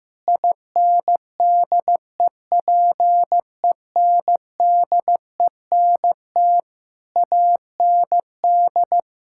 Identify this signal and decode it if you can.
Morse code, heard as a keyed tone, about 15 wpm, 700 Hz. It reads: INDEPENDENT AND
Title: Sentences from Top 1000 Words - Review Part 6 - 15wpm